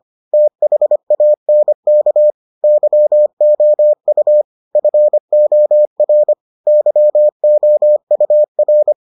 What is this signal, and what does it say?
Morse code, heard as a keyed tone, about 25 wpm, 600 Hz. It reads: THANK YOU FOR YOUR